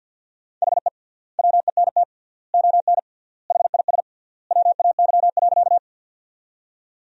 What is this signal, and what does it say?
Morse code, heard as a keyed tone, about 50 wpm, 700 Hz. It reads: HE WENT ON HIS WAY.
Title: 3x Word-Spacing - Sentences from Top 200 Words - 50wpm